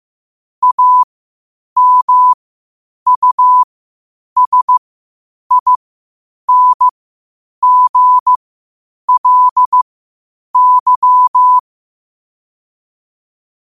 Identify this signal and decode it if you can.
Morse code, heard as a keyed tone, about 15 wpm, spaced as for 8 wpm, 1000 Hz. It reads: AMUSINGLY
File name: qotdfiles / 8wpm